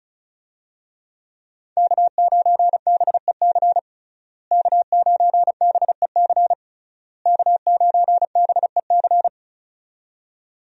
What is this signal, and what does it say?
Morse code, heard as a keyed tone, about 34 wpm, 700 Hz. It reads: K9BEC K9BEC K9BEC
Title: USA Call Signs - 5 Characters - Repeated 3x - 35wpm